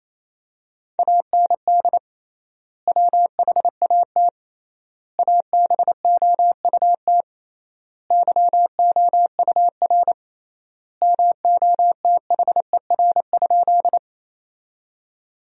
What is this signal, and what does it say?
Morse code, heard as a keyed tone, about 28 wpm, 700 Hz. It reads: AND WHAT ABOUT YOUR MOTHER?
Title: 3x Word-Spacing - Sentences from Top 200 Words - 28wpm